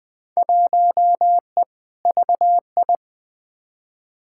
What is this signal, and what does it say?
Morse code, heard as a keyed tone, about 21 wpm, 700 Hz. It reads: 1E VI
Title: Field Day Exchanges - 20wpm